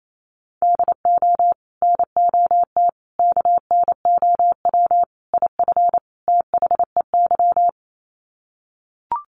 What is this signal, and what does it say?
Morse code, heard as a keyed tone, about 28 wpm, 700 Hz. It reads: DO NOT KNOW IF THEY
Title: Sets of 5 Words Encore 28wpm